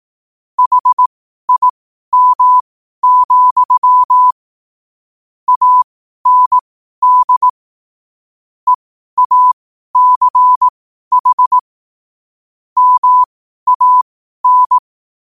Morse code, heard as a keyed tone, about 18 wpm, spaced as for 12 wpm, 1000 Hz. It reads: HIM, AND EACH MAN
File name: qotdfiles / 12wpm